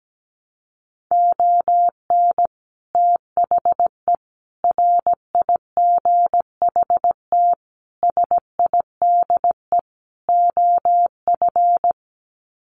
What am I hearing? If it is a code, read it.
Morse code, heard as a keyed tone, about 17 wpm, 700 Hz. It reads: ON THE RIGHT SIDE OF
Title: Sets of 5 Words Encore 17wpm